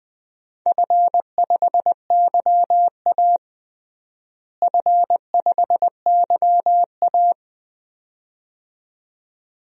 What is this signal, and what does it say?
Morse code, heard as a keyed tone, about 20 wpm, 700 Hz. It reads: F5YA F5YA
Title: DXCC-100 Call Signs - 4 Characters - Repeated 2x - 20wpm